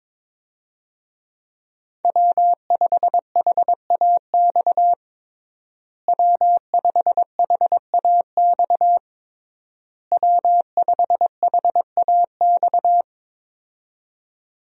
Morse code, heard as a keyed tone, about 22 wpm, 700 Hz. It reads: W5HAX W5HAX W5HAX
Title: USA Call Signs - 5 Characters - Repeated 3x - 22wpm